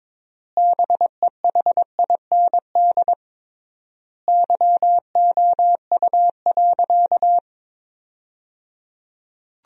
Morse code, heard as a keyed tone, about 22 wpm, 700 Hz. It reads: BEHIND YOU.
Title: 3x Word-Spacing - Sentences from Top 500 Words - 22wpm